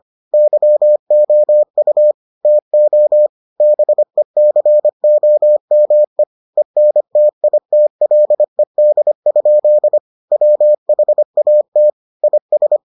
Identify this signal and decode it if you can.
Morse code, heard as a keyed tone, about 25 wpm, 600 Hz. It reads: YOU TO BECOME ENTITLED? WHAT IS